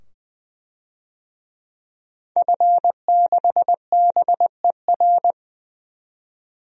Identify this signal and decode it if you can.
Morse code, heard as a keyed tone, about 20 wpm, 700 Hz. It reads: F6BER
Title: DXCC-100 Call Signs - 5 Characters - 20wpm